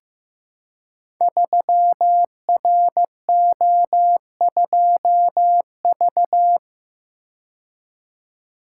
Morse code, heard as a keyed tone, about 15 wpm, 700 Hz. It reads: 3RO2V
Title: Code Groups - 5 Letter:Numbers - 15wpm